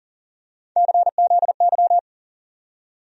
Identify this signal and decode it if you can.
Morse code, heard as a keyed tone, about 40 wpm, 700 Hz. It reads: CZY
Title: USA Call Sign Postfixes - 3 Letters 40wpm